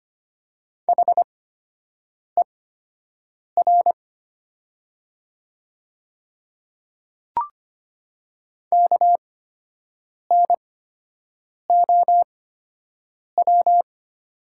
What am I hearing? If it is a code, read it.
Morse code, heard as a keyed tone, about 25 wpm, spaced as for 7 wpm, 700 Hz. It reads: HER KNOW
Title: 8x Character Spacing - Top 200 Words - 25wpm